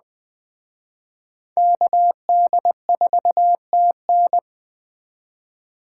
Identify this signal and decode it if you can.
Morse code, heard as a keyed tone, about 21 wpm, 700 Hz. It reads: KD4TN
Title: Speed Racing - USA Call Signs 17wpm